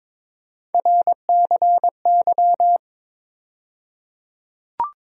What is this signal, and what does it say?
Morse code, heard as a keyed tone, about 21 wpm, 700 Hz. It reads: RCY
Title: USA Call Sign Postfixes - 3 Letters 22wpm